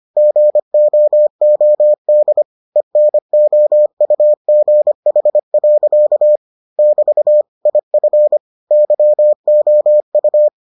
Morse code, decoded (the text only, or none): GOOD ENOUGH. <BT> IF YOU